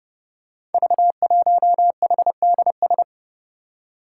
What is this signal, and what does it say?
Morse code, heard as a keyed tone, about 30 wpm, 700 Hz. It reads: V1HDS
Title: Five Challenging Letter-Number Combinations 30wpm